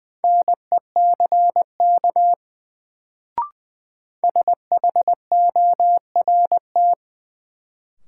Morse code, heard as a keyed tone, about 21 wpm, 700 Hz. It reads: NECK SHORT
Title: Sets of 5 Words Encore - New Word Drill - 20wpm